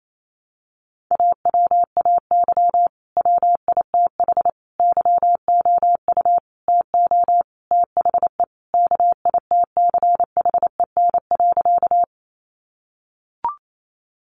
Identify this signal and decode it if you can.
Morse code, heard as a keyed tone, about 28 wpm, 700 Hz. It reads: AWAY WITH YOU TO THE KITCHEN.
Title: Sentences from Top 1400 Words 28wpm